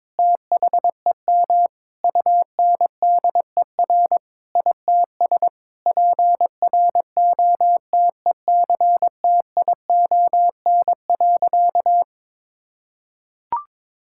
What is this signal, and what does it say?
Morse code, heard as a keyed tone, about 22 wpm, 700 Hz. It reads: THEM UNDER ITS PROTECTION.
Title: Sentences from Top 1600 Words 22wpm